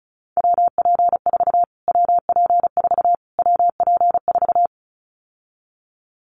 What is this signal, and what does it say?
Morse code, heard as a keyed tone, about 34 wpm, 700 Hz. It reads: WP4 WP4 WP4
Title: Callsign Prefixes for Top 25 DXCC - Repeated 3x - 35wpm